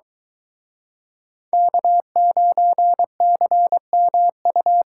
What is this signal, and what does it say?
Morse code, heard as a keyed tone, about 22 wpm, 700 Hz. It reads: K9CMU